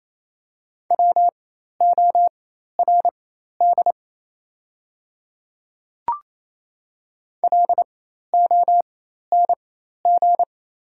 Morse code, heard as a keyed tone, about 28 wpm, spaced as for 13 wpm, 700 Hz. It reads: WORD LONG
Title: 4x Character Spacing - Top 100 Words - 28wpm